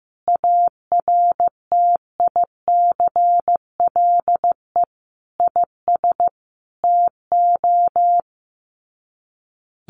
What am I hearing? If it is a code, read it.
Morse code, heard as a keyed tone, about 15 wpm, 700 Hz. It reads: ARTICLE IS TO
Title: Sets of 5 Words Encore2 15wpm